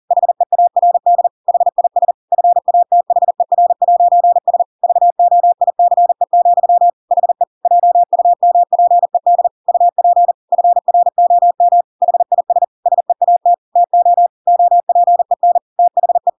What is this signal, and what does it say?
Morse code, heard as a keyed tone, about 40 wpm, 700 Hz. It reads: HEARD HIS FATHER'S VOICE, HE JUMPED UP FROM HIS SEAT TO OPEN THE
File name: Pinocchio / Pinocchio ch7 40 40 ik1hge